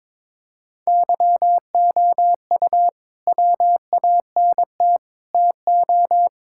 Morse code, heard as a keyed tone, about 22 wpm, 700 Hz. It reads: YOU WANT TO
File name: practice / 22wpm / Sets of 6 Words - 22wpm